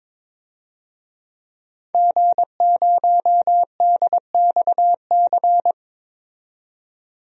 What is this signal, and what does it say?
Morse code, heard as a keyed tone, about 22 wpm, 700 Hz. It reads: G0DXC
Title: DXCC-100 Call Signs - 5 Characters - 22wpm